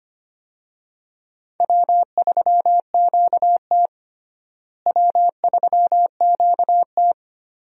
Morse code, heard as a keyed tone, about 25 wpm, 700 Hz. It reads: W3QT W3QT
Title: USA Call Signs - 4 Characters - Repeated 2x - 25wpm